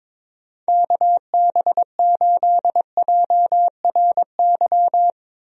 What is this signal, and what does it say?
Morse code, heard as a keyed tone, about 22 wpm, 700 Hz. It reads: KB8JRY
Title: USA Call Signs - 6 Characters - Repeated 2x - 22wpm